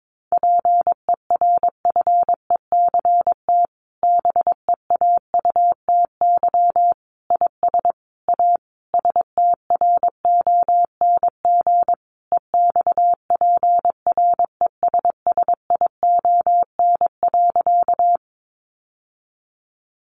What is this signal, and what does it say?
Morse code, heard as a keyed tone, about 22 wpm, 700 Hz. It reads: PERFECT BEAUTY IS A STRONG EXPRESSION.
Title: Sentences from Top 1200 Words 22wpm